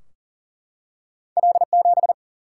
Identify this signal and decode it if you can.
Morse code, heard as a keyed tone, about 38 wpm, 700 Hz. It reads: L7
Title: Single-Letter Single-Number 40wpm